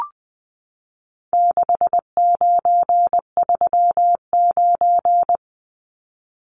Morse code, heard as a keyed tone, about 20 wpm, 700 Hz. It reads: 6939
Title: Numbers - 4-Digit 20wpm